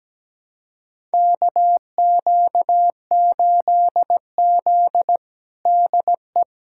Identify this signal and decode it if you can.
Morse code, heard as a keyed tone, about 17 wpm, 700 Hz. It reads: KQ8Z DE
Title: USA Callsign DE Callsign - 4 Characters - 17wpm